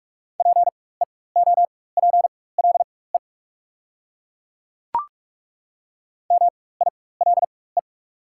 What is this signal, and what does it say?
Morse code, heard as a keyed tone, about 42 wpm, spaced as for 21 wpm, 690 Hz. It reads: PEOPLE MILE
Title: 4x Character Spacing - Top 300 Words - 45wpm